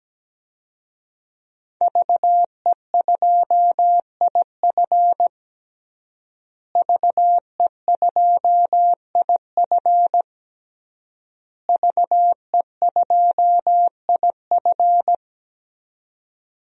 Morse code, heard as a keyed tone, about 17 wpm, 700 Hz. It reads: VE2IF VE2IF VE2IF
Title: DXCC-100 Call Signs - 5 Characters - Repeated 3x - 17wpm